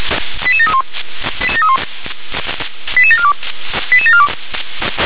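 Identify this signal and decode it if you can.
Five-tone selective call (ZVEI), heard as quick runs of five tones, tones of about 70 ms. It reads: 88731 88742 88742
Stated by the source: Five-tone ZVEI selective call 88783, received by SDR.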